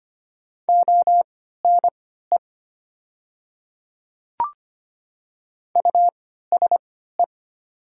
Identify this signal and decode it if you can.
Morse code, heard as a keyed tone, about 25 wpm, spaced as for 14 wpm, 700 Hz. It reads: ONE USE
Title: 3x Character Spacing - Top 100 Words - 25wpm